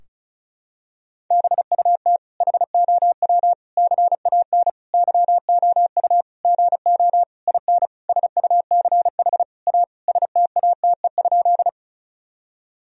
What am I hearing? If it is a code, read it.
Morse code, heard as a keyed tone, about 34 wpm, 700 Hz. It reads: BUT HOW CAN YOU GO IN SUCH A STATE?
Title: Sentences from Top 1000 Words - Review Part 2 - 35wpm